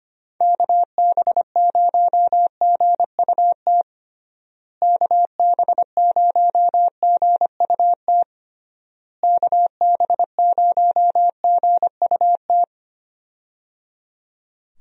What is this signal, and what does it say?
Morse code, heard as a keyed tone, about 25 wpm, 700 Hz. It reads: KB0GUT KB0GUT KB0GUT
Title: USA Call Signs - 6 Characters - Repeated 3x - 25wpm